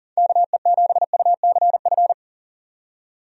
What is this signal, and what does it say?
Morse code, heard as a keyed tone, about 40 wpm, 700 Hz. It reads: KE7UCF
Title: USA Call Signs - 6 Characters - 40wpm